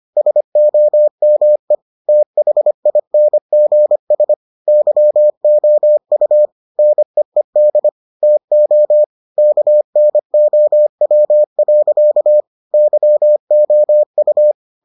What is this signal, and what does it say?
Morse code, heard as a keyed tone, about 25 wpm, 600 Hz. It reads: SOME THINGS YOU NEED TO KNOW. YOU